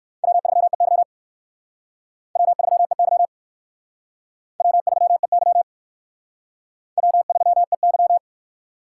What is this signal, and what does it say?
Morse code, heard as a keyed tone, about 60 wpm, 700 Hz. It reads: W3EY W3EY W3EY W3EY